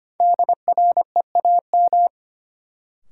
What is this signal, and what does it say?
Morse code, heard as a keyed tone, about 25 wpm, 700 Hz. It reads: DREAM
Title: Sentences from Top 1500 Words - New Word Drill Alt - 25wpm